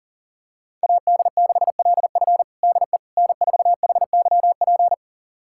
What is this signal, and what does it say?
Morse code, heard as a keyed tone, about 40 wpm, 700 Hz. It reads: AD6LF DE N4HYP
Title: USA Callsign DE Callsign - 5 Characters - 40wpm